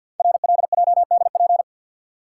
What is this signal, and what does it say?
Morse code, heard as a keyed tone, about 48 wpm, 690 Hz. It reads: RLJDP
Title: Code Groups - 5 Letters - 50wpm